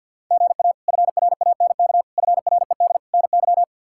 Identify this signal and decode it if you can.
Morse code, heard as a keyed tone, about 51 wpm, 700 Hz. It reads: GA FRANK FRED NY
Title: K1USN SST QSO Practice - 50wpm